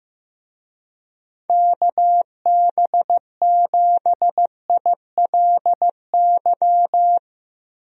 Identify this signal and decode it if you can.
Morse code, heard as a keyed tone, about 15 wpm, 700 Hz. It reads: KB7ILY
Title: USA Call Signs - 6 Characters - Repeated 3x - 15wpm